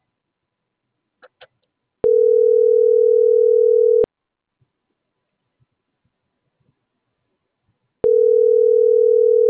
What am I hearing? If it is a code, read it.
Ringback tone (North America), 440 + 480 Hz.